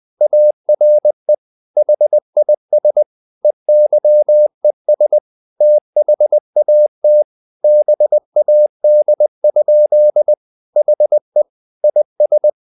Morse code, heard as a keyed tone, about 20 wpm, 600 Hz. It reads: ARE HIS EYES THAT BAD? HE IS
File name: the old man and the sea / the old man and the sea part 1 20wpm